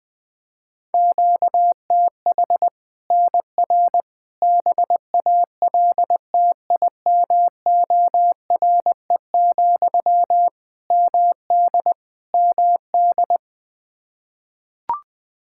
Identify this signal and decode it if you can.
Morse code, heard as a keyed tone, about 20 wpm, 700 Hz. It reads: QTH NR BALTIMORE, MD MD
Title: Practice QSO - Exchange 1 - Beginning - 20wpm